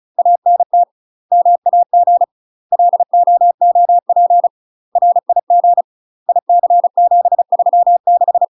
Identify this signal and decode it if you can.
Morse code, heard as a keyed tone, about 36 wpm, 700 Hz. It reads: ANT MAG LOOP RIG IC736
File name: UK MASTER QSOs TRAINING / 35WPM / UK MASTER QSOs 0010